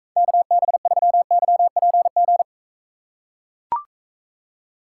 Morse code, heard as a keyed tone, about 44 wpm, 700 Hz. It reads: KB3YPG